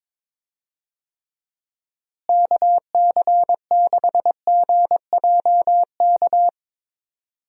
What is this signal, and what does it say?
Morse code, heard as a keyed tone, about 22 wpm, 700 Hz. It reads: KC6GJK
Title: USA Call Signs - 6 Characters - Repeated 2x - 22wpm